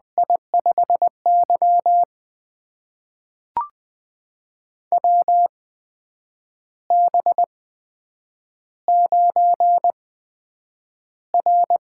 Morse code, heard as a keyed tone, about 20 wpm, spaced as for 8 wpm, 700 Hz. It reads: I5Y WB9R
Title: USA Call Signs - 8x Character Spacing - 20wpm